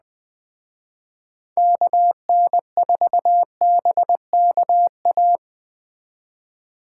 Morse code, heard as a keyed tone, about 21 wpm, 700 Hz. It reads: KN4BKA